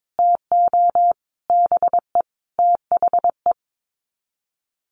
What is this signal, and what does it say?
Morse code, heard as a keyed tone, about 22 wpm, 700 Hz. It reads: TO BE THE
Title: Sets of 5 Words 22wpm